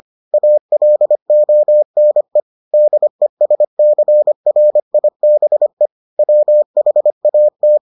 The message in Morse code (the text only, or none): ALONE DESCRIBE WHAT